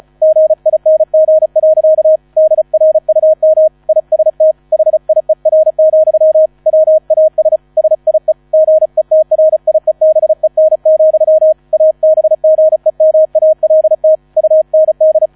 Morse code, heard as a keyed tone, about 35 wpm, 630 Hz. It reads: GING. DRUM IST HIER, WAS SIE GETRIEBEN, ABGEMALT UND